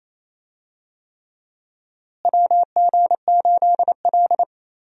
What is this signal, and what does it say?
Morse code, heard as a keyed tone, about 28 wpm, 700 Hz. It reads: WG8L
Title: USA Call Signs - 4 Characters - Repeated 3x - 28wpm